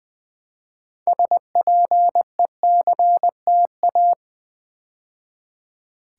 Morse code, heard as a keyed tone, about 20 wpm, 700 Hz. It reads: SPECTA
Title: Build A Word - 3 Letters At A Time - 9-Letter Words - 20wpm